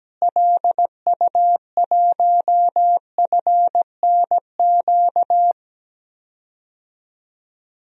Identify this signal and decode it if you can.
Morse code, heard as a keyed tone, about 17 wpm, 700 Hz. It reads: LU1FNQ
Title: DXCC-100 Call Signs - 6 Characters - Repeated 2x - 17wpm